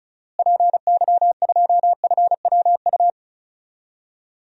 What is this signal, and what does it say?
Morse code, heard as a keyed tone, about 34 wpm, 700 Hz. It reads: PY2FWU
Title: DXCC-100 Call Signs - 6 Characters - Repeated 2x - 35wpm